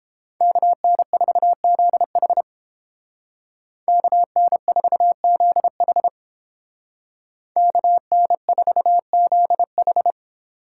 Morse code, heard as a keyed tone, about 29 wpm, 700 Hz. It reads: KN4ZH KN4ZH KN4ZH